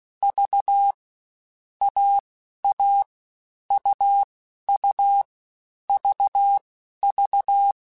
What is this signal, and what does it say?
Morse code, heard as a keyed tone, about 16 wpm, 800 Hz. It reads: V A A U U V V